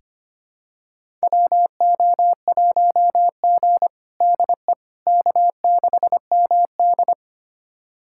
Morse code, heard as a keyed tone, about 25 wpm, 700 Hz. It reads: WO1G DE K6MD
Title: USA Callsign DE Callsign - 4 Characters - 25wpm